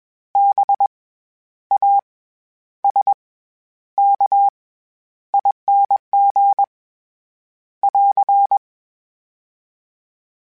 Morse code, heard as a keyed tone, about 21 wpm, 800 Hz. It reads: B A S K ING <AR>
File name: Special INGs-mp3s / Special INGs-mp3s / SlowToFaster ing @20